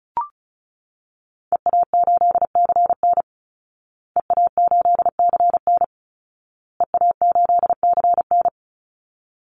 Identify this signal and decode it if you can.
Morse code, heard as a keyed tone, about 34 wpm, 700 Hz. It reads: EA8CN EA8CN EA8CN
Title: Call Signs Repeated 2x - 35wpm